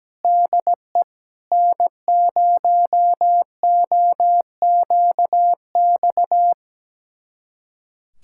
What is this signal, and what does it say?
Morse code, heard as a keyed tone, about 17 wpm, 700 Hz. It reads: DE N0OQX
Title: USA Callsign DE Callsign - 5 Characters - 17wpm